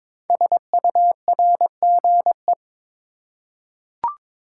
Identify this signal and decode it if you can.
Morse code, heard as a keyed tone, about 22 wpm, 700 Hz. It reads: SURGE
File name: 5 Letter Words - First 3 Letters Given - 22wpm